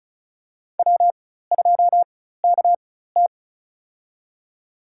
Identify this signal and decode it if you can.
Morse code, heard as a keyed tone, about 34 wpm, spaced as for 16 wpm, 700 Hz. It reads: W2KT
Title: USA Call Signs - 4 Characters - 4x Character Spacing - 35wpm